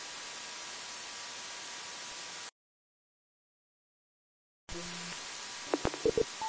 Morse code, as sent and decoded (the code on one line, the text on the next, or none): none